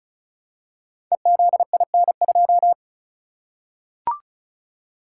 Morse code, heard as a keyed tone, about 36 wpm, 700 Hz. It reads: EZIN2
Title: Code Groups - 5 Letter:Numbers - 35wpm